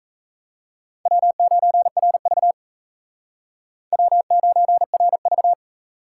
Morse code, heard as a keyed tone, about 35 wpm, 700 Hz. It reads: W9RV W9RV
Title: Speed Racing - USA 1x2 Call Signs 28wpm